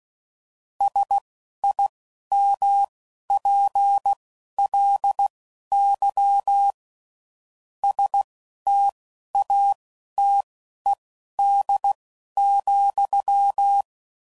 Morse code, heard as a keyed tone, about 16 wpm, spaced as for 11 wpm, 800 Hz. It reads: SIMPLY STATED,
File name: CW Code Groups @10WPM / Technical Subjects / XL